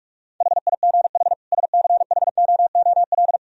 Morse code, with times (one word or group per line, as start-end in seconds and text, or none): 0.4-1.3 HIGH
1.5-3.4 SCHOOL